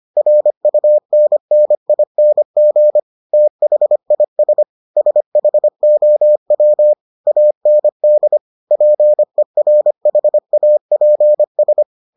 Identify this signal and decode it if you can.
Morse code, heard as a keyed tone, about 25 wpm, 600 Hz. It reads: RUNNING THIS SHOW AND PERHAPS